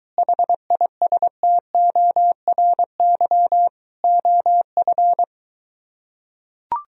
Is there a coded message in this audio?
Yes: Morse code from 0.2 to 5.2 s, heard as a keyed tone, about 23 wpm, 700 Hz. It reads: HISTORY OF